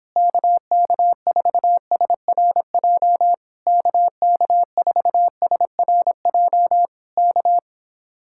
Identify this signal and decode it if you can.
Morse code, heard as a keyed tone, about 26 wpm, 700 Hz. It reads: KK4SRJ KK4SRJ K